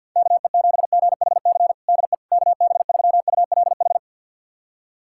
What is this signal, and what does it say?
Morse code, heard as a keyed tone, about 47 wpm, 700 Hz. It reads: KE7GHC DE KB3ULH
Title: USA Callsign DE Callsign - 6 Characters - 50wpm